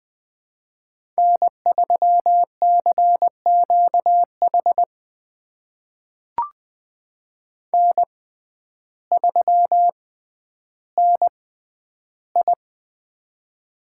Morse code, heard as a keyed tone, about 20 wpm, spaced as for 11 wpm, 700 Hz. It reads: N3CQH N3NI
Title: USA Call Signs - 6x Character Spacing - 20wpm